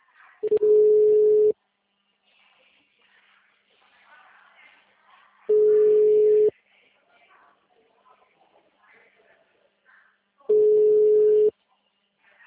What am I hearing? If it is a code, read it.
Ringback tone (Europe, 425 Hz), with speech.